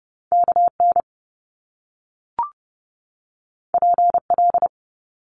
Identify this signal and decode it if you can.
Morse code, heard as a keyed tone, about 30 wpm, 700 Hz. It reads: K N P L
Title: Top Two Letter Combinations in Words 30wpm